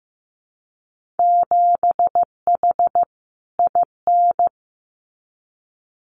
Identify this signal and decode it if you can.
Morse code, heard as a keyed tone, about 15 wpm, 700 Hz. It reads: 7H IN
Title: Winter Field Day Exchanges Encore 15wpm